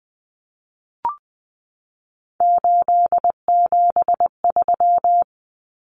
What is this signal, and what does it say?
Morse code, heard as a keyed tone, about 20 wpm, 700 Hz. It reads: 873
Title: Common Cut Numbers - 3-Digit 20wpm